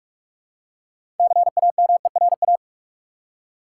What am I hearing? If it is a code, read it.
Morse code, heard as a keyed tone, about 44 wpm, 700 Hz. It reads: CAMERA
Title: Six Letter Words - 45wpm